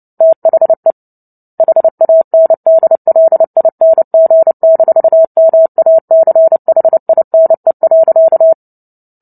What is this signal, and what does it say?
Morse code, heard as a keyed tone, about 30 wpm, 650 Hz. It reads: THE HANDLING-MACHINE.
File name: Book Two / 03-Chapter 3 - 29 WPM